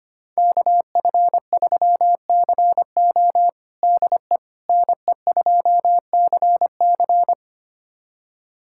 Morse code, heard as a keyed tone, about 25 wpm, 700 Hz. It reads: KF3CO DE NE2CC
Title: USA Callsign DE Callsign 25wpm